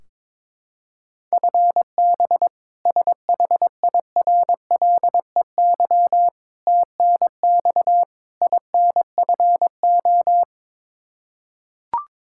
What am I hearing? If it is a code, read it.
Morse code, heard as a keyed tone, about 22 wpm, 700 Hz. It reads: FB SHIRLEY TNX INFO
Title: K1USN SST QSO Practice 22wpm